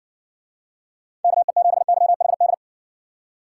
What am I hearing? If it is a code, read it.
Morse code, heard as a keyed tone, about 58 wpm, 700 Hz. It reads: KE7YHD